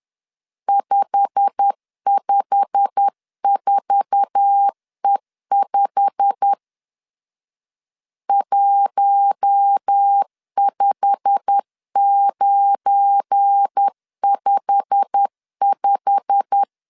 Morse code, heard as a keyed tone, about 10 wpm, 800 Hz. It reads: 554E5 15955
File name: LEZIONI / 8 YJ45 / 8 YJ45 10wpm 2549